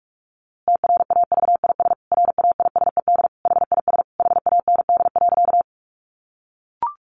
Morse code, heard as a keyed tone, about 44 wpm, 700 Hz. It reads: TRAVIS RAISED HIS HAND.
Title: A Sound Of Thunder 45wpm